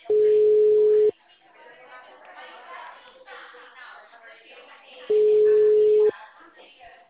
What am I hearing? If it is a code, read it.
Ringback tone (Europe, 425 Hz), with speech.